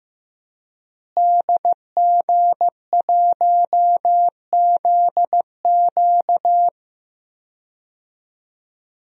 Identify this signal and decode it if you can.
Morse code, heard as a keyed tone, about 15 wpm, 700 Hz. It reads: DG1ZQ
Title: DXCC-100 Call Signs - 5 Characters - Repeated 2x - 15wpm